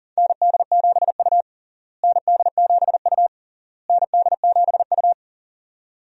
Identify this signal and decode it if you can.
Morse code, heard as a keyed tone, about 40 wpm, 700 Hz. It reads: ND7U ND7U ND7U
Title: USA Call Signs - 4 Characters - Repeated 3x - 40wpm